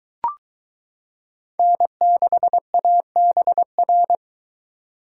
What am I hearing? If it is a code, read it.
Morse code, heard as a keyed tone, about 21 wpm, 700 Hz. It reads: N6ABR